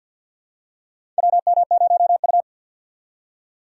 Morse code, heard as a keyed tone, about 47 wpm, 700 Hz. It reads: WM0U
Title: USA Call Signs - 4 Characters - 2x Character Spacing - 50wpm